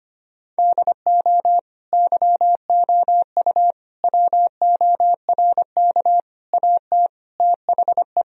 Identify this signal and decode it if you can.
Morse code, heard as a keyed tone, about 25 wpm, 700 Hz. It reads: DO YOU WORK AT THE
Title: Sentences from Top 1000 Words - Review Part 2 - 25wpm